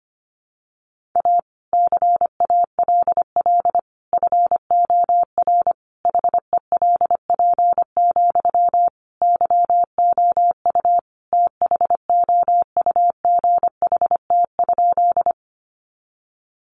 Morse code, heard as a keyed tone, about 25 wpm, 700 Hz. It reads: A CALL FOR HELP, YOU THOUGHT?
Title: Sentences from Top 300 Words 25wpm